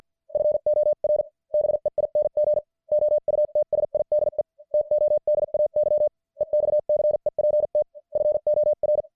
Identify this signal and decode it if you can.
Morse code, heard as a keyed tone, about 50 wpm, spaced as for 44 wpm, 600 Hz. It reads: FOR BEING OUTSIDE TODAY EXCEPT FOR
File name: archived / CWBros